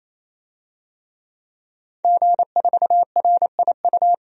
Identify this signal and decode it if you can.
Morse code, heard as a keyed tone, about 28 wpm, 700 Hz. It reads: G4RIU
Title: DXCC-100 Call Signs - 5 Characters - 28wpm